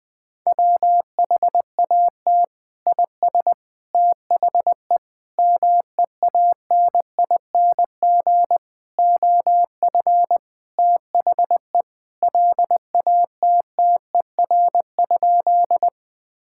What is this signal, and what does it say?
Morse code, heard as a keyed tone, about 21 wpm, 700 Hz. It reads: WHAT IS THE MEANING OF THE LATTER?
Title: Sentences from Top 1600 Words 20wpm